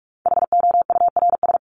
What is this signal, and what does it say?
Morse code, heard as a keyed tone, about 44 wpm, 700 Hz. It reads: HOURS
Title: ES:S Suffix - Top 100 Words with Base Words 45wpm